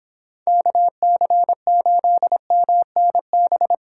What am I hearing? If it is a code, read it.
Morse code, heard as a keyed tone, about 29 wpm, 700 Hz. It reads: KC8MNB